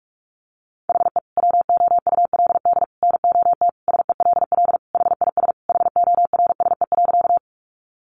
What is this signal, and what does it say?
Morse code, heard as a keyed tone, about 44 wpm, 700 Hz. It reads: HE WOULD NOT SELL HIS HORSE.
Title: Sentences from Top 1100 Words 45wpm